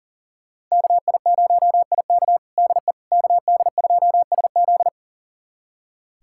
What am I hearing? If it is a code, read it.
Morse code, heard as a keyed tone, about 40 wpm, 700 Hz. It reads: KI0IK DE KD2SZ
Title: USA Callsign DE Callsign - 5 Characters - 40wpm